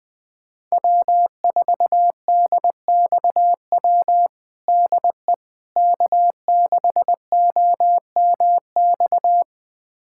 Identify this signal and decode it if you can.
Morse code, heard as a keyed tone, about 20 wpm, 700 Hz. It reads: W4DXW DE K6OMX
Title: USA Callsign DE Callsign - 5 Characters - 20wpm